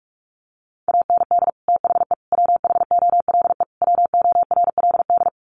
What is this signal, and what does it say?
Morse code, heard as a keyed tone, about 44 wpm, 700 Hz. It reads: AND THE WHOLE WORLD
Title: Sentences from Top 1300 Words 45wpm